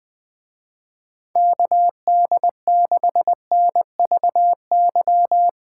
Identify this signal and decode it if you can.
Morse code, heard as a keyed tone, about 20 wpm, 700 Hz. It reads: KD6NVY
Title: Speed Racing - USA 2x3 Call Signs 17wpm